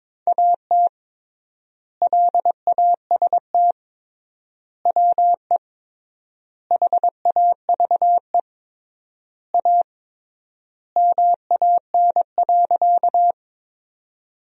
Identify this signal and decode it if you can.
Morse code, heard as a keyed tone, about 22 wpm, 700 Hz. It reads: AT LAST WE HAVE A MAN.
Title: 3x Word-Spacing - Sentences from Top 300 Words - 22wpm